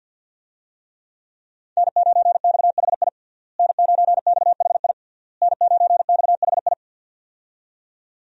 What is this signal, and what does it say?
Morse code, heard as a keyed tone, about 47 wpm, 700 Hz. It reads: N9XHI N9XHI N9XHI
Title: USA Call Signs - 5 Characters - Repeated 3x - 50wpm